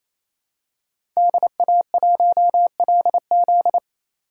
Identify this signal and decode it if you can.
Morse code, heard as a keyed tone, about 28 wpm, 700 Hz. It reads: DA1LZ
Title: USA Call Signs - 3x Character Spacing - 28wpm